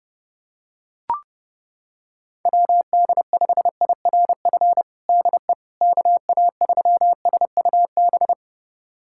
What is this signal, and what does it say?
Morse code, heard as a keyed tone, about 30 wpm, 700 Hz. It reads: WD5IRF DE KA3SUB
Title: USA Callsign DE Callsign - 6 Characters - 30wpm